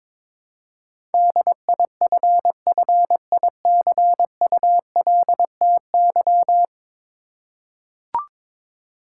Morse code, heard as a keyed tone, about 22 wpm, 700 Hz. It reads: DIFFICULTY
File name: practice / 22wpm / 10 Letter Words - First 3 Letters Given - 22wpm